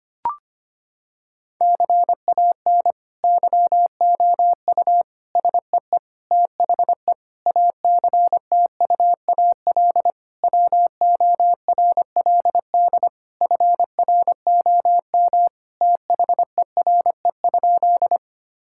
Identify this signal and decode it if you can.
Morse code, heard as a keyed tone, about 25 wpm, 700 Hz. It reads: CAN YOU SEE THE ACTUAL WORLD FROM THERE?
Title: Sentences from Top 1000 Words - Review Part 5 - 25wpm